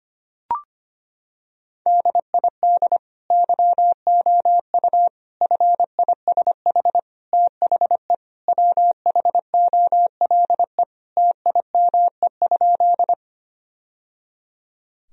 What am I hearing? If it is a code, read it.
Morse code, heard as a keyed tone, about 25 wpm, 700 Hz. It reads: DID YOU FISH THE WHOLE TIME?
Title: Sentences from Top 1000 Words - Review Part 1 - 25wpm